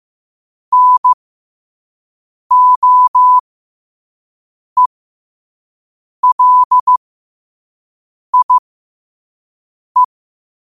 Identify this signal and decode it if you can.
Morse code, heard as a keyed tone, about 15 wpm, spaced as for 5 wpm, 1000 Hz. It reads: NOELIE
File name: qotdfiles / 5wpm